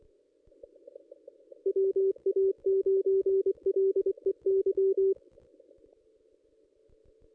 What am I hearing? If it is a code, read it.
Morse code, heard as a keyed tone, about 24 wpm, 390 Hz. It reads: WA9LEY
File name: contests / ss-cw-k5tr-op-n5rz -1800